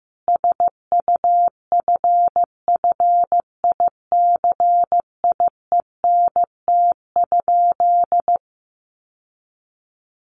Morse code, heard as a keyed tone, about 15 wpm, 700 Hz. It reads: SUFFICIENT?
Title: Sentences from Top 1400 Words 15wpm